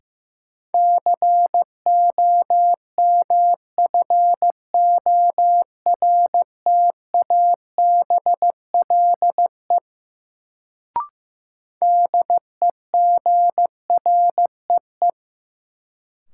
Morse code, heard as a keyed tone, about 15 wpm, 700 Hz. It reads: COMFORTABLE DEGREE
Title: Sentences from Top 1500 Words - New Word Drill Alt - 15wpm